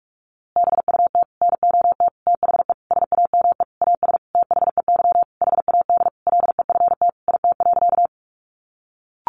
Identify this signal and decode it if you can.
Morse code, heard as a keyed tone, about 44 wpm, 700 Hz. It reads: BUT NOT THE SAME AS THEY HAD LEFT IT.
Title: A Sound Of Thunder 45wpm